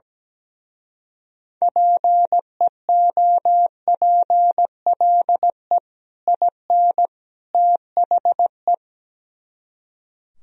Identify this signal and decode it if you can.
Morse code, heard as a keyed tone, about 18 wpm, 700 Hz. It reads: PEOPLE IN THE
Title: Speed-Racing - Sets of 3 Words 17wpm